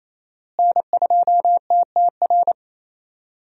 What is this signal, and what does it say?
Morse code, heard as a keyed tone, about 27 wpm, 700 Hz. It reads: N2TTR
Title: USA Call Signs - 5 Characters - Repeated 2x - 28wpm